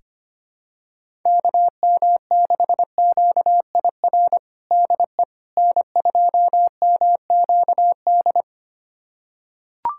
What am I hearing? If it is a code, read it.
Morse code, heard as a keyed tone, about 25 wpm, 700 Hz. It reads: KM6QIR DE N2MQD
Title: USA Callsign DE Callsign 25wpm